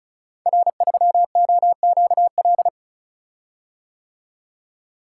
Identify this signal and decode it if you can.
Morse code, heard as a keyed tone, about 36 wpm, 700 Hz. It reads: R3OQL
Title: Code Groups - 5 Letter:Numbers - 35wpm